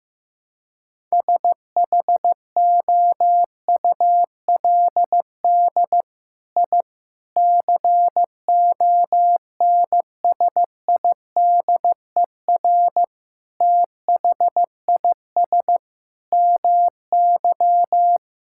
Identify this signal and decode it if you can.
Morse code, heard as a keyed tone, about 15 wpm, 700 Hz. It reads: SHOULD I CONSIDER THIS MY
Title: Sentences from Top 1000 Words - Review Part 5 - 15wpm